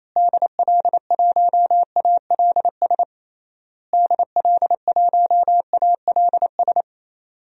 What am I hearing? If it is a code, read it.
Morse code, heard as a keyed tone, about 28 wpm, 700 Hz. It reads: DL1ALS DL1ALS
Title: DXCC-100 Call Signs - 6 Characters - Repeated 3x - 28wpm